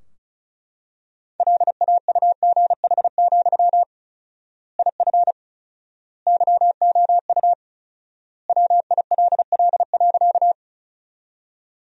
Morse code, heard as a keyed tone, about 34 wpm, 700 Hz. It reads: LAUGH, IF YOU WILL.
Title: 4x Word-Spacing - Sentences from Top 500 Words - 35wpm